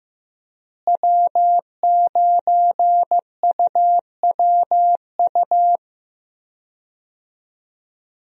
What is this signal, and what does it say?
Morse code, heard as a keyed tone, about 15 wpm, 700 Hz. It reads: W9UWU
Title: USA Call Signs - 5 Characters - Repeated 2x - 15wpm